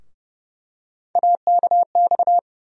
Morse code, heard as a keyed tone, about 30 wpm, 700 Hz. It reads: AKX
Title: USA Call Sign Postfixes - 3 Letters 30wpm